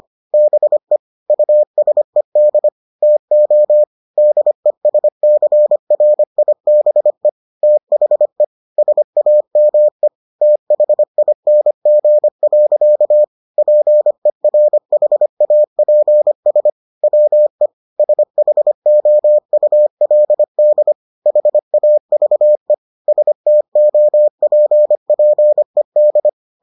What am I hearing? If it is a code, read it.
Morse code, heard as a keyed tone, about 25 wpm, 600 Hz. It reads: BE USED TO DESCRIBE THE SAME THING. PERHAPS WE SHOULD HAVE STOPPED